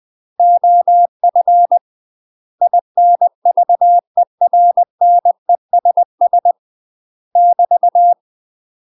Morse code, heard as a keyed tone, about 20 wpm, 700 Hz. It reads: OF INVERNESS <BT>
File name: UK MASTER QSOs TRAINING / 20WPM / UK MASTER QSOs 0049